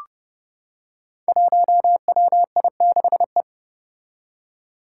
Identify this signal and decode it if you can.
Morse code, heard as a keyed tone, about 29 wpm, 700 Hz. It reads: 1WI6E
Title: Five Challenging Letter-Number Combinations 30wpm